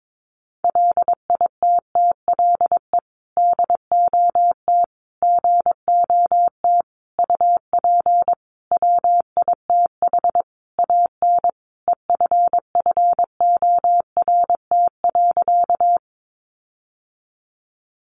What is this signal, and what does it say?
Morse code, heard as a keyed tone, about 22 wpm, 700 Hz. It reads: LITTLE DOT GOT UP WITH AN EFFORT.
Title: Sentences from Top 1500 Words 22wpm